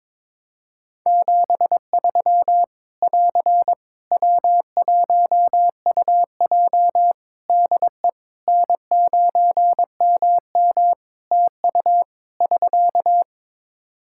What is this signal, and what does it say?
Morse code, heard as a keyed tone, about 22 wpm, 700 Hz. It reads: 73 <AR> W1UJ DE N9MM TU <SK>
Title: Practice QSO - Exchange 4 - Ending - 22wpm